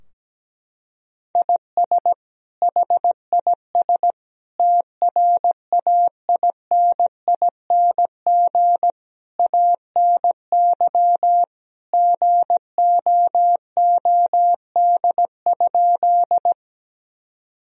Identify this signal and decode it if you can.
Morse code, heard as a keyed tone, about 17 wpm, 700 Hz. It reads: IS HIS TRAINING ANY GOOD?
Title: Sentences from Top 1000 Words - Review Part 4 - 17wpm